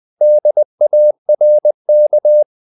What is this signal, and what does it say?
Morse code, heard as a keyed tone, about 20 wpm, 600 Hz. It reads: DARK
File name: the old man and the sea part 2 20wpm